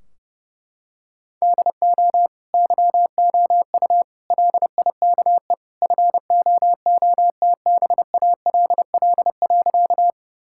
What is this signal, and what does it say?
Morse code, heard as a keyed tone, about 32 wpm, 700 Hz. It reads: DO YOU LIKE FOOTBALL.
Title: Sentences from Top 1600 Words 30wpm